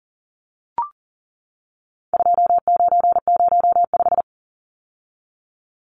Morse code, heard as a keyed tone, about 40 wpm, 700 Hz. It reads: 2905
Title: Numbers - 4-Digit 40wpm